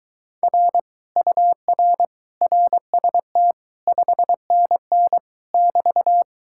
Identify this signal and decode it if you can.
Morse code, heard as a keyed tone, about 24 wpm, 700 Hz. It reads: R UR RST 5NN <BT>